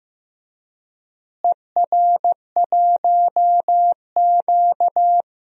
Morse code, heard as a keyed tone, about 15 wpm, 700 Hz. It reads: ER1Q
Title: DXCC-100 Call Signs - 4 Characters - Repeated 3x - 15wpm